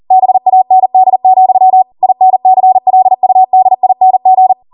Morse code, heard as a keyed tone, about 40 wpm, 750 Hz. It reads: BAND, INCLUDING